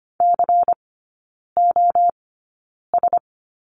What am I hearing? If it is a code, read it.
Morse code, heard as a keyed tone, about 25 wpm, spaced as for 9 wpm, 700 Hz. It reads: COS
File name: course / 25:10wpm / 093 Lesson 16e - Letter c - Three Characters 25wpm